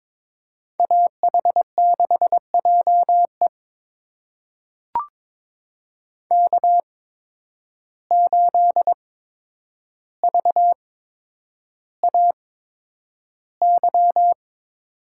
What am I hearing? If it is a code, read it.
Morse code, heard as a keyed tone, about 23 wpm, spaced as for 10 wpm, 700 Hz. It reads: AH6JE K8VAY
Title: USA Call Signs - 5 Characters - 8x Character Spacing - 22wpm